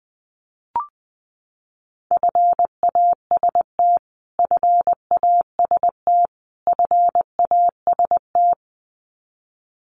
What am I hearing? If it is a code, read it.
Morse code, heard as a keyed tone, about 20 wpm, 700 Hz. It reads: FAST FAST FAST
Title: Top 500 Words Repeated 20wpm